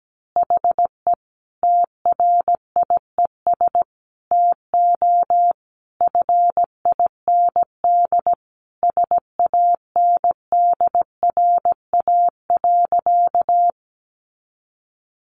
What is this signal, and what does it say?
Morse code, heard as a keyed tone, about 17 wpm, 700 Hz. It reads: HE TRIES TO FIND SANDRA.
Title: Easy Phrases 17wpm